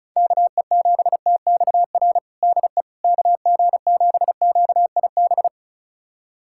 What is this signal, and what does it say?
Morse code, heard as a keyed tone, about 36 wpm, 700 Hz. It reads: KE7TXR DE KG7QIB
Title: USA Callsign DE Callsign - 6 Characters - 35wpm